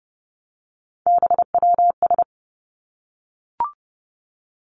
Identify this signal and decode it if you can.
Morse code, heard as a keyed tone, about 30 wpm, 700 Hz. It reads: BWS
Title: Three Challenging Letter-Number Combinations 30wpm